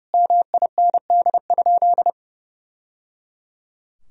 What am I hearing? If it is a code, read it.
Morse code, heard as a keyed tone, about 30 wpm, 700 Hz. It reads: MIND?
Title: Sentences from Top 1000 Words - Review Part 3 - 30wpm